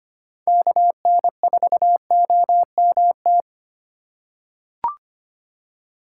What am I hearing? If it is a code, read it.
Morse code, heard as a keyed tone, about 25 wpm, 700 Hz. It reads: KN4OMT
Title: USA Call Signs - 6 Characters - 4x Character Spacing - 25wpm